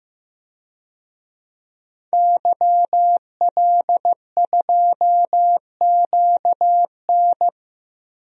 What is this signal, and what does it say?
Morse code, heard as a keyed tone, about 15 wpm, 700 Hz. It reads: YL2QN
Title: DXCC-100 Call Signs - 5 Characters - 15wpm